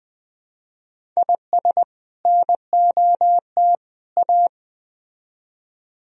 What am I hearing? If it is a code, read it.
Morse code, heard as a keyed tone, about 20 wpm, 700 Hz. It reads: IS NOT A
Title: Speed-Racing - Sets of 3 Words 17wpm